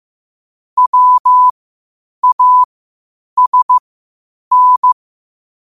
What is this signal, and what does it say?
Morse code, heard as a keyed tone, about 15 wpm, spaced as for 8 wpm, 1000 Hz. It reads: WASN
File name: qotdfiles / 8wpm